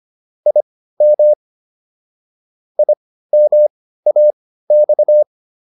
Morse code, heard as a keyed tone, about 25 wpm, spaced as for 15 wpm, 600 Hz. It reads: IM IMAX
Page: intermediate-practice-files